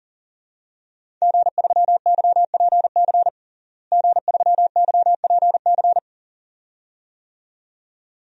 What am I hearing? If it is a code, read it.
Morse code, heard as a keyed tone, about 40 wpm, 700 Hz. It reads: G3YPC G3YPC
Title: DXCC-100 Call Signs - 5 Characters - Repeated 2x - 40wpm